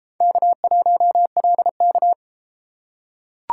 Morse code, heard as a keyed tone, about 32 wpm, 700 Hz. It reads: K1LK